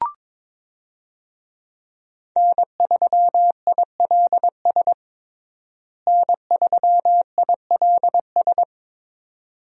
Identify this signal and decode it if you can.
Morse code, heard as a keyed tone, about 22 wpm, 700 Hz. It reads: N3ILS N3ILS
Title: USA Call Signs - 5 Characters - Repeated 2x - 22wpm